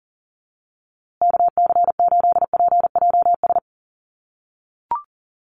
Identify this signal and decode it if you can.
Morse code, heard as a keyed tone, about 37 wpm, 700 Hz. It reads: KC8PJS
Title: USA Call Signs Repeated 3x 40wpm